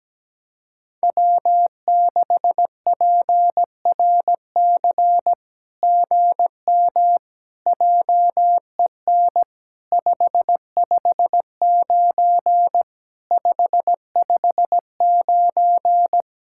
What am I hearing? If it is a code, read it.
Morse code, heard as a keyed tone, about 17 wpm, 700 Hz. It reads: W6PRC GM JEN 559 559
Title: SOTA QSO Practice 17wpm